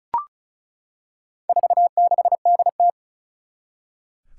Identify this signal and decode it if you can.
Morse code, heard as a keyed tone, about 36 wpm, 700 Hz. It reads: 46DT
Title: Four Challenging Letter-Number Combinations 35wpm